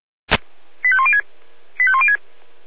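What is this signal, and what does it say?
Five-tone selective call (ZVEI), heard as quick runs of five tones, tones of about 70 ms. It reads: 85106 85106
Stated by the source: ZVEI five-tone call 85108 received on SDR.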